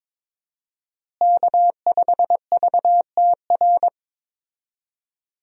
Morse code, heard as a keyed tone, about 22 wpm, 700 Hz. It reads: K5VTR
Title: USA Call Signs - 5 Characters - Repeated 2x - 22wpm